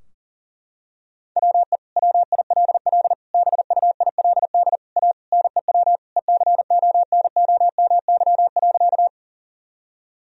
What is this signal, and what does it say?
Morse code, heard as a keyed tone, about 40 wpm, 700 Hz. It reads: WE WILL BUILD A NEW ECONOMY.
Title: Sentences from Top 1000 Words - Review Part 3 - 40wpm